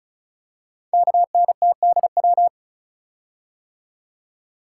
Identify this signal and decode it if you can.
Morse code, heard as a keyed tone, about 36 wpm, 690 Hz. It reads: KNTDW
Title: Code Groups - 5 Letters - 35wpm